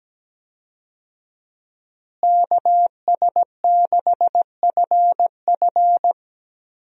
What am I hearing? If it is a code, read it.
Morse code, heard as a keyed tone, about 17 wpm, 700 Hz. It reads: KS6FF
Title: USA Call Signs - 5 Characters - Repeated 2x - 17wpm